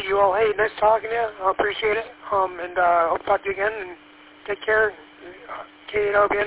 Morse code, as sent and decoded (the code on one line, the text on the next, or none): none